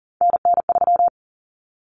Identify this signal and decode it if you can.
Morse code, heard as a keyed tone, about 40 wpm, 700 Hz. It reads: NN3
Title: Common Cut Numbers - 3-Digit 40wpm